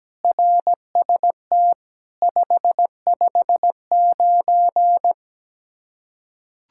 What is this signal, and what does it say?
Morse code, heard as a keyed tone, about 17 wpm, 700 Hz. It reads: RST 559
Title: Practice QSO - Exchange 1 - Beginning - 17wpm